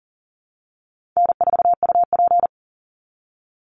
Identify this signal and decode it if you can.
Morse code, heard as a keyed tone, about 37 wpm, 700 Hz. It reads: N4UP
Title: Call Signs Repeated 3x - 40wpm